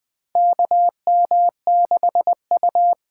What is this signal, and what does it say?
Morse code, heard as a keyed tone, about 20 wpm, 700 Hz. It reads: KM6U
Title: USA Call Signs - 6 Characters - 8x Character Spacing - 20wpm